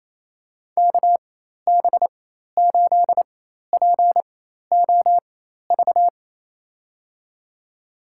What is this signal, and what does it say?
Morse code, heard as a keyed tone, about 28 wpm, spaced as for 13 wpm, 700 Hz. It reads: KB8POV
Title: USA Call Signs - 6 Characters - 4x Character Spacing - 28wpm